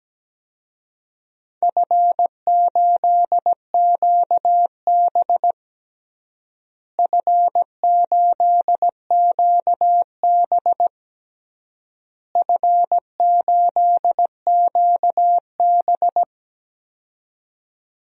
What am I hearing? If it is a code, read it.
Morse code, heard as a keyed tone, about 17 wpm, 700 Hz. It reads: F8QB F8QB F8QB
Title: DXCC-100 Call Signs - 4 Characters - Repeated 3x - 17wpm